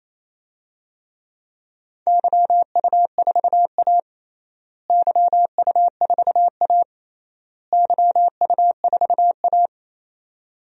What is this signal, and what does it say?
Morse code, heard as a keyed tone, about 27 wpm, 700 Hz. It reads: YU4A YU4A YU4A
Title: DXCC-100 Call Signs - 4 Characters - Repeated 3x - 28wpm